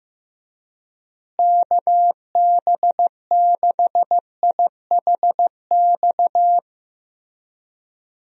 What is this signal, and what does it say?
Morse code, heard as a keyed tone, about 16 wpm, 700 Hz. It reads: KB6IHX
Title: USA Call Signs - 6 Characters - Repeated 3x - 15wpm